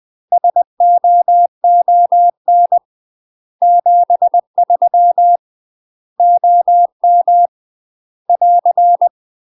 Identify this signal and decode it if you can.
Morse code, heard as a keyed tone, about 20 wpm, 700 Hz. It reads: SOON 73 OM <AR>
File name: UK MASTER QSOs TRAINING / 20WPM / UK MASTER QSOs 0010